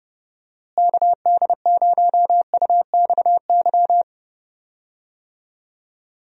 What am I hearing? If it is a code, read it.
Morse code, heard as a keyed tone, about 30 wpm, 700 Hz. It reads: KD0UXY
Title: USA Call Signs - 6 Characters - Repeated 3x - 30wpm